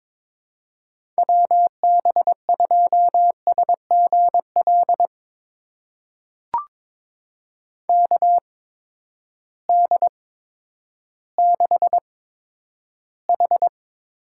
Morse code, heard as a keyed tone, about 22 wpm, spaced as for 11 wpm, 700 Hz. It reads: WB2SGL KD6H
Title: USA Call Signs - 6 Characters - 8x Character Spacing - 22wpm